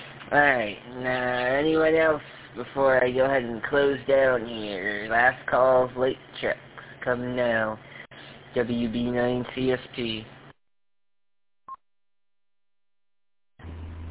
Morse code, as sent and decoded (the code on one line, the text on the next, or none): none